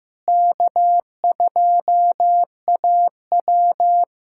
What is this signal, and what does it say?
Morse code, heard as a keyed tone, about 15 wpm, 700 Hz. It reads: K2AW
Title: USA Callsign DE Callsign - 4 Characters - 15wpm